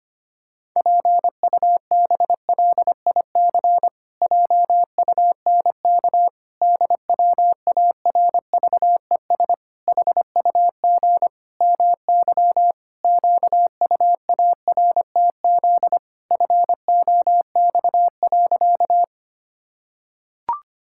Morse code, heard as a keyed tone, about 25 wpm, 700 Hz. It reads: PUBLIC JUNK DWARVES HUG MY QUARTZ FOX.
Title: Pangram Challenge v2 - 25wpm